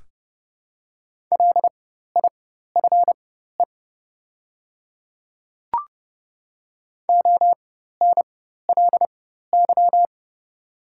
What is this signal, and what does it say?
Morse code, heard as a keyed tone, about 30 wpm, spaced as for 14 wpm, 700 Hz. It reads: LIFE ONLY
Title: 4x Character Spacing - Top 300 Words - 30wpm